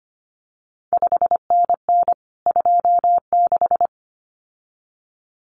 Morse code, heard as a keyed tone, about 25 wpm, 700 Hz. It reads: 5NN 26
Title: CQ WW DX Contest Exchange Encore - 25wpm